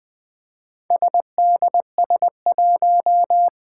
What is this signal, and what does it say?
Morse code, heard as a keyed tone, about 20 wpm, 700 Hz. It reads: SDS1
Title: Four Challenging Letter-Number Combinations 20wpm